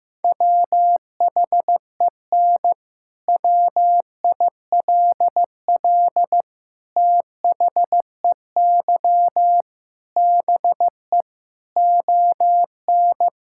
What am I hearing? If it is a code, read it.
Morse code, heard as a keyed tone, about 15 wpm, 700 Hz. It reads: WHEN WILL THEY BE ON
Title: Sentences from Top 1000 Words - Review Part 6 - 15wpm